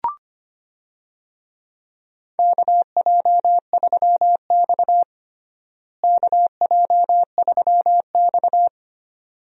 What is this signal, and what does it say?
Morse code, heard as a keyed tone, about 25 wpm, 700 Hz. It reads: KJ3X KJ3X
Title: USA Call Signs - 4 Characters - Repeated 2x - 25wpm